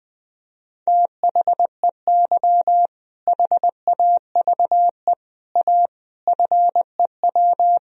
Morse code, heard as a keyed tone, about 19 wpm, 700 Hz. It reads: THEY HAVE A FEW
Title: Sentences from Top 1000 Words - Review Part 1 - 20wpm